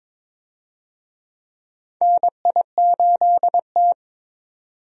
Morse code, heard as a keyed tone, about 22 wpm, 700 Hz. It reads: NI8T
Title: USA Call Signs - 4 Characters - 22wpm